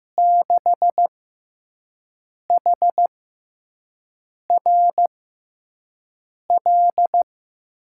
Morse code, heard as a keyed tone, about 15 wpm, spaced as for 5 wpm, 700 Hz. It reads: 6HRL
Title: USA Call Signs - 6 Characters - 6x Character Spacing - 15wpm